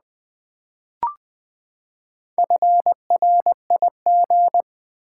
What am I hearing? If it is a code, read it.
Morse code, heard as a keyed tone, about 20 wpm, 700 Hz. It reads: FRIG
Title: Build A Word - 4 Letters At A Time - 10-Letter Words - 20wpm